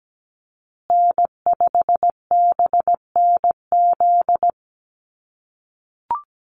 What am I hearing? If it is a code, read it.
Morse code, heard as a keyed tone, about 17 wpm, 700 Hz. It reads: N5BNZ
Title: USA Call Signs Repeated 3x 17wpm